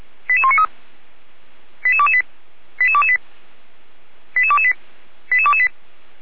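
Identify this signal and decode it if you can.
Five-tone selective call (ZVEI), heard as quick runs of five tones, tones of about 70 ms. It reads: 88193 88228 88228 88228 88228
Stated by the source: ZVEI five-tone selective call 88193 received by SDR, Feuerwehr Orlamünde.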